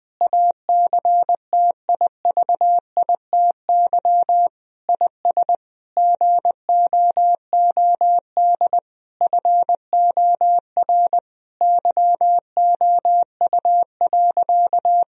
Morse code, heard as a keyed tone, about 20 wpm, 700 Hz. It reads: ACTIVITY IS GOOD FOR YOU.
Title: Sentences from Top 1000 Words - Review Part 4 - 20wpm